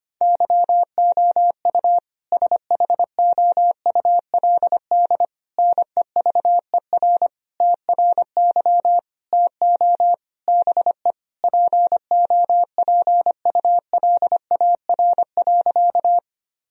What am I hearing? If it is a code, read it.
Morse code, heard as a keyed tone, about 25 wpm, 700 Hz. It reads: YOU SHOULD NEVER TRY TO BE POPULAR.
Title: Sentences from Top 1000 Words - Review Part 3 - 25wpm